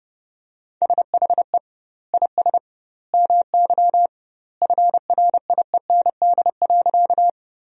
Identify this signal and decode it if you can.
Morse code, heard as a keyed tone, about 32 wpm, 700 Hz. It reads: SHE IS MY FRIEND.
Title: 2x Word-Spacing - Sentences from Top 300 Words - 30wpm